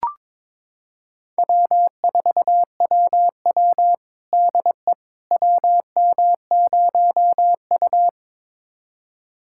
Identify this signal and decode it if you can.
Morse code, heard as a keyed tone, about 22 wpm, 700 Hz. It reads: W4WW DE WM0U
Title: USA Callsign DE Callsign - 4 Characters - 22wpm